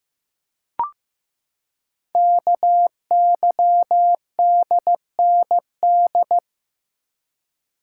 Morse code, heard as a keyed tone, about 15 wpm, 700 Hz. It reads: KYDND
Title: Code Groups - 5 Letters - 15wpm